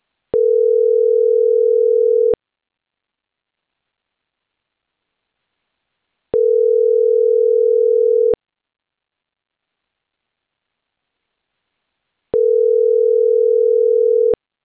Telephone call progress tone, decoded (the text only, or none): ringback tone